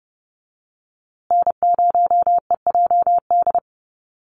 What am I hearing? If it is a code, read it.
Morse code, heard as a keyed tone, about 30 wpm, 700 Hz. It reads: N0EJD
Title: USA Call Signs Repeated 2x 30wpm